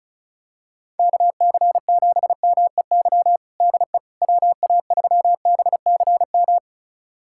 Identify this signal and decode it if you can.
Morse code, heard as a keyed tone, about 36 wpm, 700 Hz. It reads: KC7MEY DE WA3BCM
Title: USA Callsign DE Callsign 35wpm